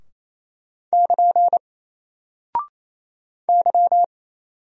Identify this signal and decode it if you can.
Morse code, heard as a keyed tone, about 28 wpm, 700 Hz. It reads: <KN> Y
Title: Single Character (Letter:Number:Punctuation:Prosign) 28wpm